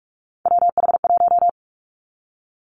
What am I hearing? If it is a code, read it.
Morse code, heard as a keyed tone, about 46 wpm, 700 Hz. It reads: WH1
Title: Three Challenging Letter-Number Combinations 45wpm